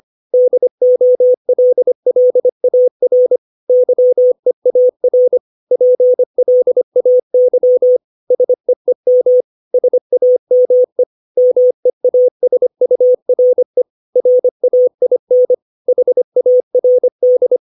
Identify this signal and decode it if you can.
Morse code, heard as a keyed tone, about 25 wpm, 500 Hz. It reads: DOLLAR YEAR PLAY SEEM SAME MEASURE RAIN HARD
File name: 632 words / 632 Words 25WPM